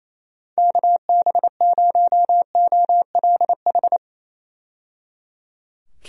Morse code, heard as a keyed tone, about 27 wpm, 700 Hz. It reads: KB0OLH
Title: USA Call Signs - 6 Characters - 28wpm